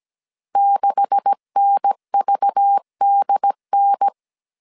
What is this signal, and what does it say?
Morse code, heard as a keyed tone, about 17 wpm, 800 Hz. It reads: 6NVDN
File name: CW / DET / 1 9 1500C S C 16wpm 2909